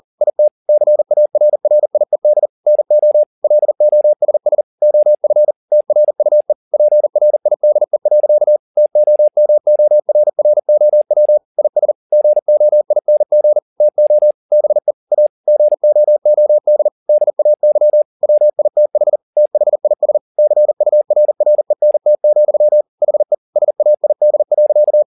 Morse code, heard as a keyed tone, about 40 wpm, 600 Hz. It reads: IT CARRIED NO LOSS OF TRUE PRIDE. TOMORROW IS GOING TO BE A GOOD DAY WITH THIS CURRENT, HE SAID.
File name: the old man and the sea 1 40wpm